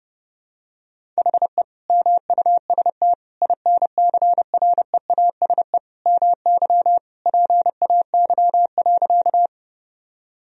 Morse code, heard as a keyed tone, about 30 wpm, 700 Hz. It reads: HE MUST INCREASE MY PAY.
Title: Sentences from Top 1000 Words - Review Part 4 - 30wpm